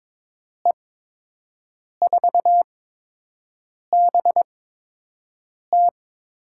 Morse code, heard as a keyed tone, about 21 wpm, spaced as for 6 wpm, 700 Hz. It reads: E4BT
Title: USA Call Signs - 5 Characters - 8x Character Spacing - 22wpm